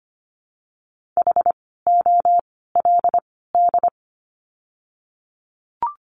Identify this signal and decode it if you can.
Morse code, heard as a keyed tone, about 25 wpm, 700 Hz. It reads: H O L D
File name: course / 25:10wpm / 094 Lesson 16f - Letter c - Words 25wpm